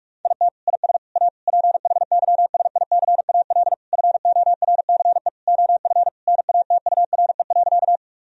Morse code, heard as a keyed tone, about 42 wpm, 700 Hz. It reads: IT IS A PHYSICAL FORCE OF NATURE.
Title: Sentences from Top 1000 Words - Review Part 4 - 45wpm